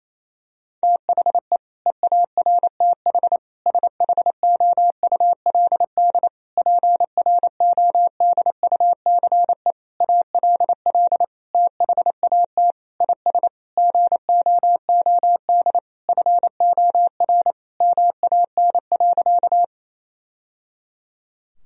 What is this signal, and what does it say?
Morse code, heard as a keyed tone, about 27 wpm, 700 Hz. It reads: THE EARTH SHOULD PRODUCE ALL THAT IS GOOD FOR MAN.
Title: Sentences from Top 1000 Words - Review Part 3 - 28wpm